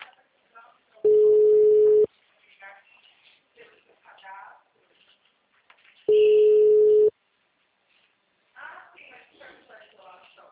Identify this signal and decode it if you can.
Ringback tone (Europe, 425 Hz), with speech.